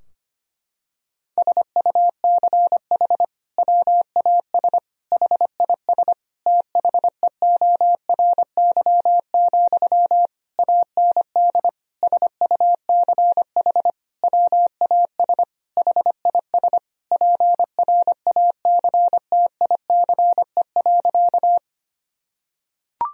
Morse code, heard as a keyed tone, about 25 wpm, 700 Hz. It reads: SUCH WAS HIS THEORY, AND SUCH WAS HIS PRACTICE.
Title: Sentences from Top 1000 Words - Review Part 4 - 25wpm